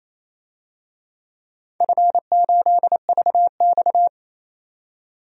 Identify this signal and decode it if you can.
Morse code, heard as a keyed tone, about 28 wpm, 700 Hz. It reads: F8VX
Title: DXCC-100 Call Signs - 4 Characters - 28wpm